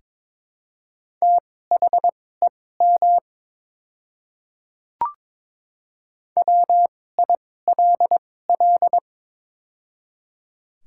Morse code, heard as a keyed tone, about 22 wpm, spaced as for 16 wpm, 700 Hz. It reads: THEM WILL
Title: 2x Character Spacing - Top 100 Words - 22wpm